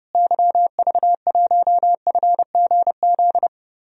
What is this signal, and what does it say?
Morse code, heard as a keyed tone, about 31 wpm, 700 Hz. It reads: YV1FGZ
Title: DXCC-100 Call Signs - 6 Characters - Repeated 3x - 30wpm